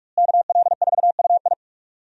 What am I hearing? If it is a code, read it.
Morse code, heard as a keyed tone, about 42 wpm, 700 Hz. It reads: KL4UI
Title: USA Call Signs - 5 Characters - Repeated 3x - 45wpm